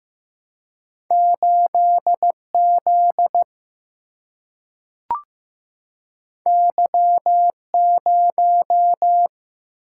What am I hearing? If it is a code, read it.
Morse code, heard as a keyed tone, about 15 wpm, 700 Hz. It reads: 8 Z Y 0
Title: Single-Letter Single-Number 15wpm